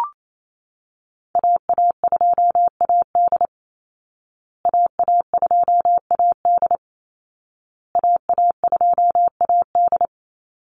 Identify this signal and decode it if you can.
Morse code, heard as a keyed tone, about 28 wpm, 700 Hz. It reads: AA2AD AA2AD AA2AD
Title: Call Signs Repeated 2x - 28wpm